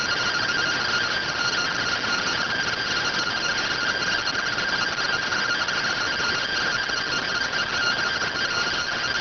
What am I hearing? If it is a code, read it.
Two-tone teleprinter signal, about 170 Hz shift.